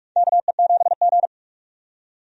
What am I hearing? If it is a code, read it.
Morse code, heard as a keyed tone, about 42 wpm, 700 Hz. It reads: KE7G
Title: USA Call Signs - 4 Characters - 45wpm